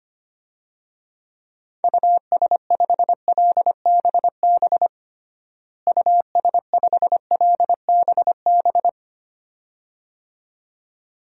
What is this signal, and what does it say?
Morse code, heard as a keyed tone, about 25 wpm, 700 Hz. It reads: US5LBB US5LBB
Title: DXCC-100 Call Signs - 6 Characters - Repeated 2x - 25wpm